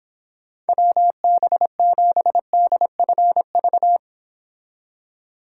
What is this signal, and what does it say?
Morse code, heard as a keyed tone, about 31 wpm, 700 Hz. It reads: WB7DFV